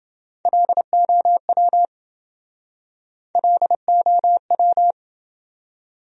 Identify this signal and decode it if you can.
Morse code, heard as a keyed tone, about 28 wpm, 700 Hz. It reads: LOW LOW